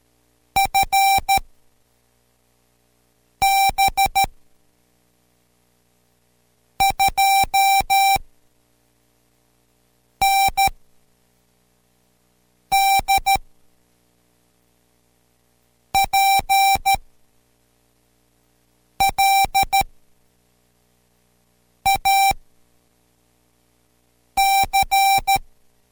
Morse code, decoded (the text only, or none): FB 2ND PLAC